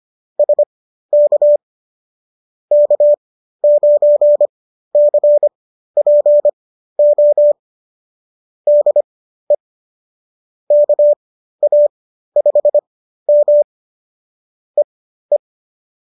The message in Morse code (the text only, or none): SK K9CPO DE KA5M EE